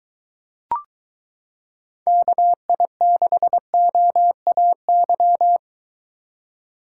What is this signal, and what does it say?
Morse code, heard as a keyed tone, about 23 wpm, 700 Hz. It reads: KI6OAY